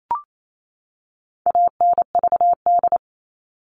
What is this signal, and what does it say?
Morse code, heard as a keyed tone, about 28 wpm, 700 Hz. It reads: ANVD
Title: All Cut Numbers - 4-Digit 28wpm